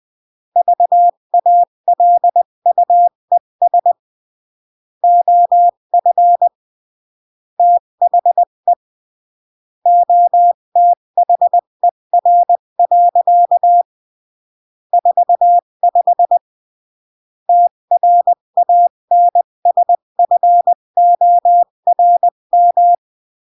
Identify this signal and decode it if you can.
Morse code, heard as a keyed tone, about 20 wpm, 700 Hz. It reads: VALUES OF THE OTHER. 45 TRANSFORM
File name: CW / jodorowsky / 82 commandments 20-15